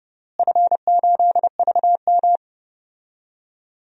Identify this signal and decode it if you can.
Morse code, heard as a keyed tone, about 30 wpm, 700 Hz. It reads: F8VM
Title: DXCC-100 Call Signs - 4 Characters - Repeated 3x - 30wpm